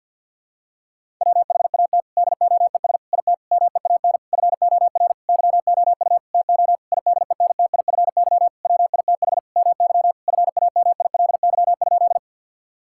Morse code, heard as a keyed tone, about 50 wpm, 700 Hz. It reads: WHAT DOES IT MEAN FOR YOU TO IDENTIFY WITH MY FAMILY?
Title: Sentences from Top 1000 Words - Review Part 6 - 50wpm